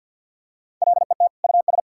Morse code, heard as a keyed tone, about 50 wpm, 700 Hz. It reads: LET US
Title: Sentences from Top 1000 Words - Review Part 6 - 50wpm